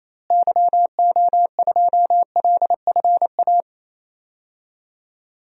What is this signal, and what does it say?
Morse code, heard as a keyed tone, about 28 wpm, 700 Hz. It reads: YO2LFA
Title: DXCC-100 Call Signs - 6 Characters - Repeated 2x - 28wpm